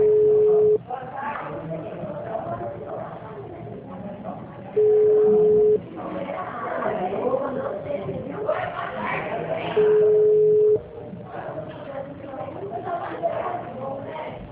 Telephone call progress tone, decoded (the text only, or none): ringback tone